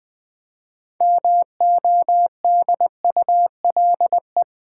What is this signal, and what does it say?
Morse code, heard as a keyed tone, about 20 wpm, 700 Hz. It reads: MODULE
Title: 6 Letter Words - First 3 Letters Given - 20wpm